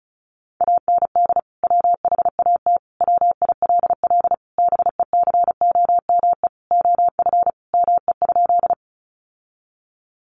Morse code, heard as a keyed tone, about 34 wpm, 700 Hz. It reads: AND WHAT WILL BECOME OF ME?
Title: Sentences from Top 600 Words 35wpm